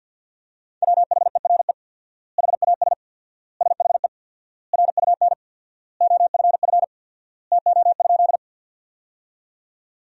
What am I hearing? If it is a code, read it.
Morse code, heard as a keyed tone, about 52 wpm, 700 Hz. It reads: WHERE HAS SHE RUN OFF TO?
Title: 4x Word-Spacing - Sentences from Top 300 Words - 50wpm